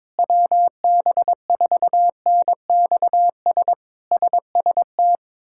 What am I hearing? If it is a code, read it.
Morse code, heard as a keyed tone, about 22 wpm, 700 Hz. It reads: WB4NXS SST
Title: K1USN SST QSO Practice - 22wpm